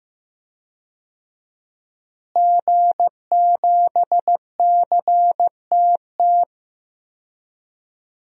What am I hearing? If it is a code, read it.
Morse code, heard as a keyed tone, about 15 wpm, 700 Hz. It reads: G7CTT
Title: DXCC-100 Call Signs - 5 Characters - Repeated 3x - 15wpm